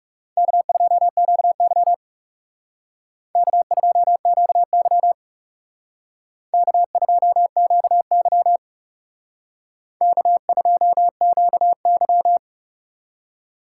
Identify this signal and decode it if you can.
Morse code, heard as a keyed tone, about 38 wpm, 690 Hz. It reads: K2QY K2QY K2QY K2QY